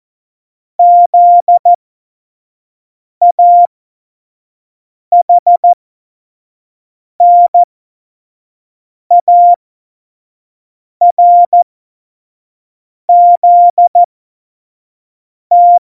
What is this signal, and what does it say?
Morse code, heard as a keyed tone, about 14 wpm, spaced as for 5 wpm, 700 Hz. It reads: ZAHNARZT
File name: Morsekurs / Witze / Witz 27. Februar 2018, 25 F70